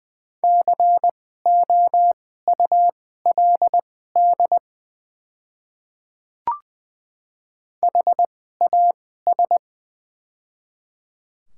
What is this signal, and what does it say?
Morse code, heard as a keyed tone, about 20 wpm, spaced as for 14 wpm, 700 Hz. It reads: COULD HAS
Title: 2x Character Spacing - Top 100 Words - 20wpm